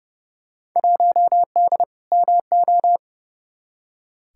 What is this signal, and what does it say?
Morse code, heard as a keyed tone, about 30 wpm, 700 Hz. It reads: 1D MO
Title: Field Day Exchanges - 30wpm